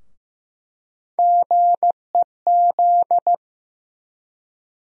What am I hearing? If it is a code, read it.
Morse code, heard as a keyed tone, about 15 wpm, 700 Hz. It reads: GEZ
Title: USA Call Sign Postfixes - 3 Letters 15wpm